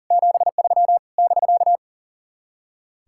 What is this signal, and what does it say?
Morse code, heard as a keyed tone, about 52 wpm, 700 Hz. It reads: 73 <BK>